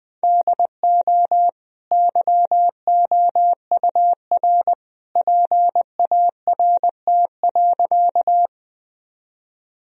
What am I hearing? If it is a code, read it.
Morse code, heard as a keyed tone, about 20 wpm, 700 Hz. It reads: DO YOUR PART.
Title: Sentences from Top 1000 Words - Review Part 1 - 20wpm